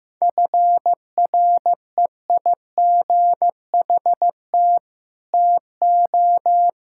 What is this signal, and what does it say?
Morse code, heard as a keyed tone, about 15 wpm, 700 Hz. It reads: FREIGHT TO
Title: Pangram Challenge v2 - 15wpm